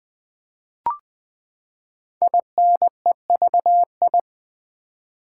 Morse code, heard as a keyed tone, about 20 wpm, 700 Hz. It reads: INEVI
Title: Build A Word - 5 Letters At A Time - 10-Letter Words - 20wpm